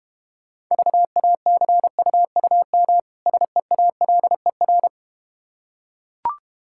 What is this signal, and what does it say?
Morse code, heard as a keyed tone, about 32 wpm, 700 Hz. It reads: VACUUM SEALER